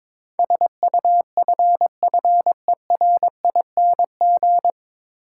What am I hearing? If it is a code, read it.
Morse code, heard as a keyed tone, about 23 wpm, 700 Hz. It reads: SUFFERING
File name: 9 Letter Words - First 3 Letters Given - 22wpm